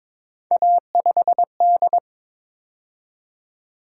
Morse code, heard as a keyed tone, about 26 wpm, 700 Hz. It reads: A5D
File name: Speed-Racing - Three Letter-Number Combinations - 22wpm